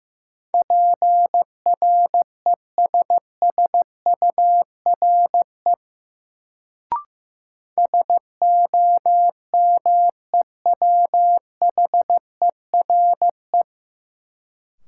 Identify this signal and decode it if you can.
Morse code, heard as a keyed tone, about 15 wpm, 700 Hz. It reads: PRESSURE SOMEWHERE
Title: Sentences from Top 1400 Words - New Word Drill Alt - 15wpm